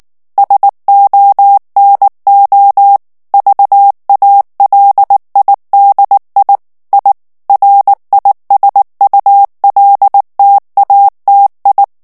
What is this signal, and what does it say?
Morse code, heard as a keyed tone, about 19 wpm, 800 Hz. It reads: SONO VALIDI I RISULTATI
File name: morse mp3 / ITALIEN texte clair 2 vitesse 18 mots minute